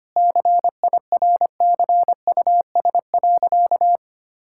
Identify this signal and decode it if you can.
Morse code, heard as a keyed tone, about 25 wpm, 700 Hz. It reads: CIRCUS.
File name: practice / 25wpm / Difficult Sentences with Hints - 25wpm